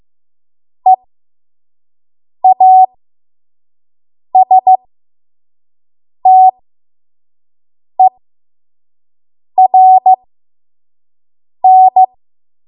Morse code, heard as a keyed tone, about 15 wpm, spaced as for 5 wpm, 750 Hz. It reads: EASTERN
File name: Morse / 05WPM